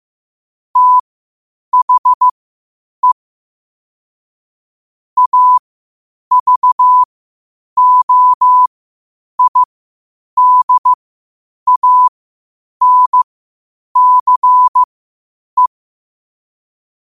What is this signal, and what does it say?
Morse code, heard as a keyed tone, about 15 wpm, spaced as for 8 wpm, 1000 Hz. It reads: THE AVOIDANCE
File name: qotdfiles / 8wpm